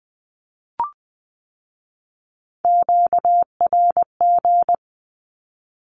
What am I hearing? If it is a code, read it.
Morse code, heard as a keyed tone, about 20 wpm, 700 Hz. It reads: QRG
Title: Q-Codes with Spoken Meaning - 20wpm